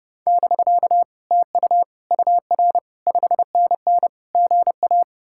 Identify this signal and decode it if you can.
Morse code, heard as a keyed tone, about 39 wpm, 700 Hz. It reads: <BK> TU UR 5NN GA